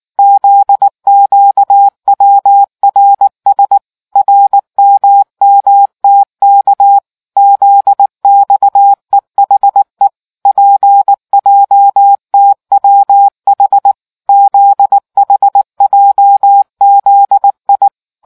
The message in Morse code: ZQWRS RMMTK ZXEHE PJTWH ZHJZI

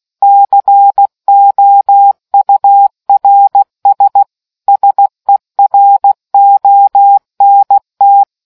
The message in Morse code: COURS SERONT

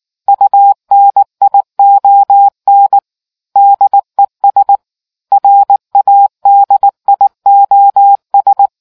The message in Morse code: UNION DES RADIOS

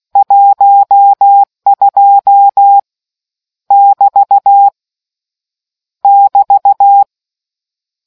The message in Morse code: 12 <BT> <BT>

